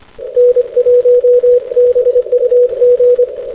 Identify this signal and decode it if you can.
Morse code, heard as a keyed tone, about 25 wpm, 490 Hz. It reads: N1BUG
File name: hc2ac- -x